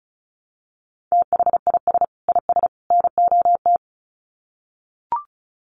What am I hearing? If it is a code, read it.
Morse code, heard as a keyed tone, about 34 wpm, 700 Hz. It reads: THIS IS NOT
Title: Sets of 3 Words 35wpm